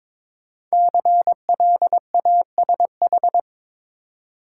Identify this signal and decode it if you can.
Morse code, heard as a keyed tone, about 23 wpm, 700 Hz. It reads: CLASH
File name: 5 Letter Words - First 3 Letters Given - 22wpm